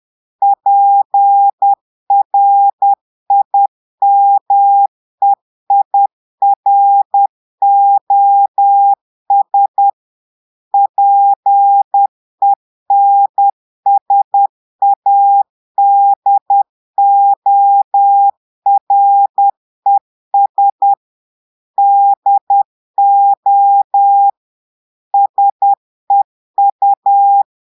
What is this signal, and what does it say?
Morse code, heard as a keyed tone, about 10 wpm, 800 Hz. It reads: PRIMEIROS PENSADORES DO SEU
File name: Portugais - texte en clair 1 - vitesse - 10 mots minute